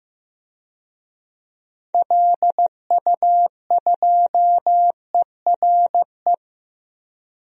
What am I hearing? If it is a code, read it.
Morse code, heard as a keyed tone, about 15 wpm, 700 Hz. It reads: LU2ERE
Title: DXCC-100 Call Signs - 6 Characters - 15wpm